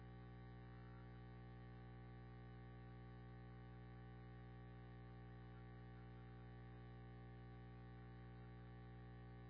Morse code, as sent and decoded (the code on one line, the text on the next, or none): none